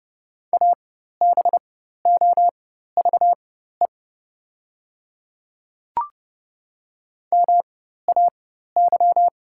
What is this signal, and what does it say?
Morse code, heard as a keyed tone, about 29 wpm, spaced as for 14 wpm, 690 Hz. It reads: ABOVE MAY
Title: 4x Character Spacing - Top 500 Words - 30wpm